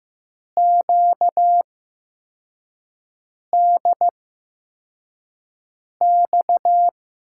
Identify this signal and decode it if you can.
Morse code, heard as a keyed tone, about 15 wpm, spaced as for 4 wpm, 700 Hz. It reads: QDX
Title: USA Call Signs - 6 Characters - 8x Character Spacing - 15wpm